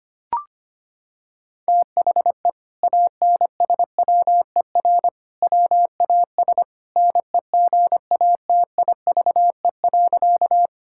Morse code, heard as a keyed tone, about 25 wpm, 700 Hz. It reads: THE ANSWER WAS NEGATIVE.
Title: Sentences from Top 1000 Words - Review Part 6 - 25wpm